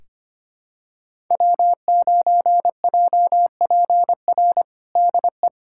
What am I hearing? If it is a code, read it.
Morse code, heard as a keyed tone, about 25 wpm, 700 Hz. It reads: W9JPR DE
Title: USA Callsign DE Callsign - 5 Characters - 25wpm